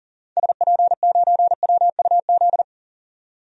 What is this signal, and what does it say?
Morse code, heard as a keyed tone, about 37 wpm, 700 Hz. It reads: SP9WUZ
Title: DXCC-100 Call Signs - 6 Characters - 40wpm